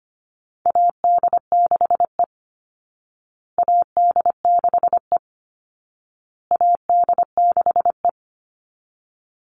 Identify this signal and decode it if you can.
Morse code, heard as a keyed tone, about 25 wpm, 700 Hz. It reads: AD6E AD6E AD6E
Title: Call Signs Repeated 2x - 25wpm